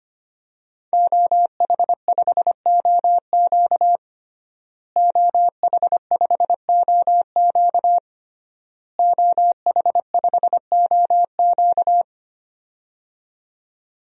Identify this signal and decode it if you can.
Morse code, heard as a keyed tone, about 25 wpm, 700 Hz. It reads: OH5OQ OH5OQ OH5OQ
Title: DXCC-100 Call Signs - 5 Characters - Repeated 3x - 25wpm